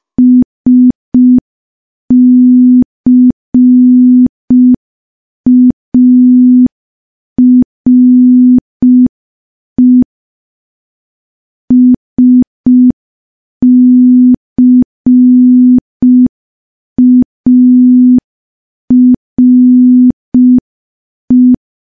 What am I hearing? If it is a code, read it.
Morse code, heard as a keyed tone, about 5 wpm, 250 Hz. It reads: SCARE SCARE
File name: morse / morse-training-05WPM